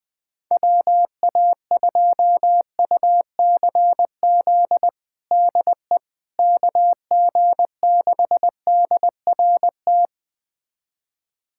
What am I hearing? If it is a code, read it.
Morse code, heard as a keyed tone, about 20 wpm, 700 Hz. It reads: WA2UCZ DE KG6DRT
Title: USA Callsign DE Callsign - 6 Characters - 20wpm